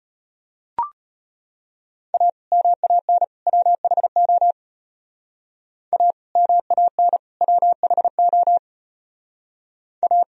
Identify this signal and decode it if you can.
Morse code, heard as a keyed tone, about 30 wpm, 700 Hz. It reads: A MAN WHO A MAN WHO A
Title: Speed-Racing - Sets of 3 Words 25wpm